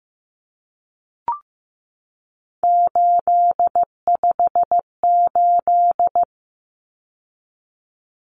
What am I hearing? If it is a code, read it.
Morse code, heard as a keyed tone, about 15 wpm, 700 Hz. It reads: 858
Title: Numbers - 3-Digit 15wpm